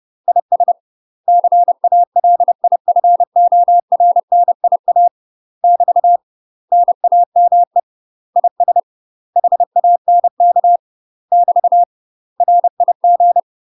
Morse code, heard as a keyed tone, about 30 wpm, 700 Hz. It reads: IS CALIFORNIA <BT> NAME IS HANK <BT> RIG
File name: UK MASTER QSOs TRAINING / 30WPM / UK MASTER QSOs 0046